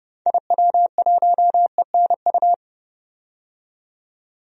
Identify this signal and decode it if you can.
Morse code, heard as a keyed tone, about 30 wpm, 700 Hz. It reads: IW1ENU
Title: DXCC-100 Call Signs - 6 Characters - 30wpm